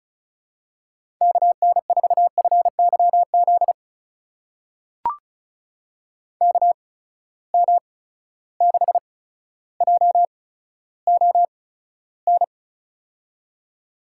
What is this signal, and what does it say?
Morse code, heard as a keyed tone, about 36 wpm, spaced as for 15 wpm, 700 Hz. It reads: KN4FYZ KM6JON
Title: USA Call Signs - 6 Characters - 8x Character Spacing - 35wpm